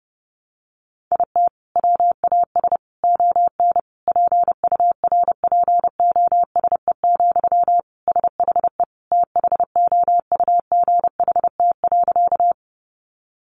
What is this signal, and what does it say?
Morse code, heard as a keyed tone, about 30 wpm, 700 Hz. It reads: IT WAS ON PURPOSE, SHE THOUGHT.
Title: Sentences from Top 800 Words 30wpm